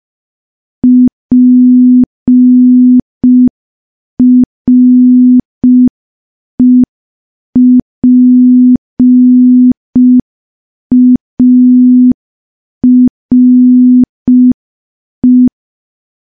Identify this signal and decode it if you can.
Morse code, heard as a keyed tone, about 5 wpm, 250 Hz. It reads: PREPARE
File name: morse / morse-training-05WPM